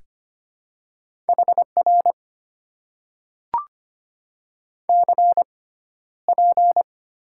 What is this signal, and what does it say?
Morse code, heard as a keyed tone, about 25 wpm, 700 Hz. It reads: H R C P
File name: Top 90 QSO Elements - 6x Character Spacing - 25wpm